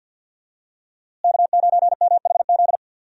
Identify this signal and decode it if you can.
Morse code, heard as a keyed tone, about 50 wpm, 700 Hz. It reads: K9MHZ
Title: USA Call Signs - 5 Characters - Repeated 2x - 50wpm